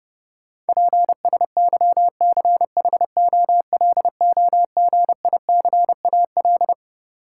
Morse code, heard as a keyed tone, about 32 wpm, 700 Hz. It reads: PSYCHOLOGICAL
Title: Sentences from Top 1400 Words - New Word Drill - 30wpm